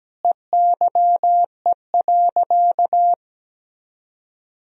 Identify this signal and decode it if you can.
Morse code, heard as a keyed tone, about 17 wpm, 700 Hz. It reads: EYE.
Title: Sentences from Top 1000 Words - Review Part 3 - 17wpm